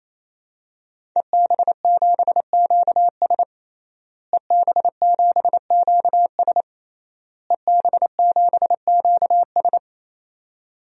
Morse code, heard as a keyed tone, about 28 wpm, 700 Hz. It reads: EB7QS EB7QS EB7QS
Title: DXCC-100 Call Signs - 5 Characters - Repeated 3x - 28wpm